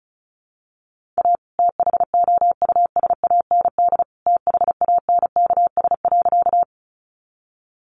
Morse code, heard as a keyed tone, about 34 wpm, 700 Hz. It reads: A THOUSAND THANKS.
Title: Sentences from Top 600 Words 35wpm